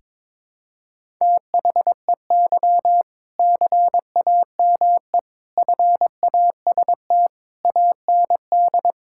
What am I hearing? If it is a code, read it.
Morse code, heard as a keyed tone, about 22 wpm, 700 Hz. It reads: THEY CAME FAST AND
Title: Sentences from Top 1600 Words 22wpm